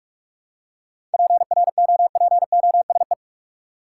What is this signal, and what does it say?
Morse code, heard as a keyed tone, about 46 wpm, 700 Hz. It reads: PROPOSE
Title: Seven Letter Words - 45wpm